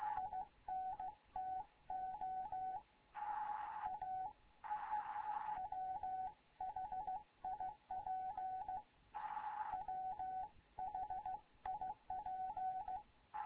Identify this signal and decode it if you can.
Morse code, heard as a keyed tone, about 15 wpm, 750 Hz, with radio noise. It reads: INTO A WHIP WHIP
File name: Morze-HA2MN / hangp / 2 11 GYAKORLAT